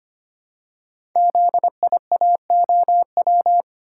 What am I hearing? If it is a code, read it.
Morse code, heard as a keyed tone, about 25 wpm, 700 Hz. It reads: ZIAOW
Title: Code Groups - 5 Letter:Numbers - 25wpm